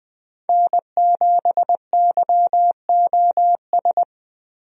Morse code, heard as a keyed tone, about 20 wpm, 700 Hz. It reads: N7YOS
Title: USA Call Signs - 5 Characters - Repeated 2x - 20wpm